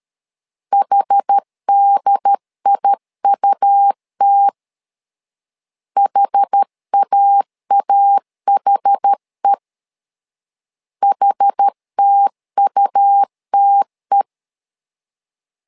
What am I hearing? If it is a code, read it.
Morse code, heard as a keyed tone, about 13 wpm, 800 Hz. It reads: HDIUT HAAHE HTUTE
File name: LEZIONI / 2 UHDN / 2 UHD 12wpm 1540